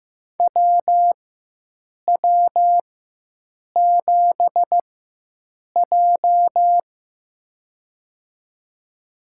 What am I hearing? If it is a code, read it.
Morse code, heard as a keyed tone, about 15 wpm, spaced as for 7 wpm, 700 Hz. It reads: WW7J
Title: USA Call Signs - 4x Character Spacing - 15wpm